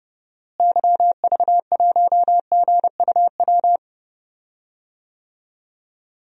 Morse code, heard as a keyed tone, about 32 wpm, 700 Hz. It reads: YV1GUW
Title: DXCC-100 Call Signs - 6 Characters - Repeated 3x - 30wpm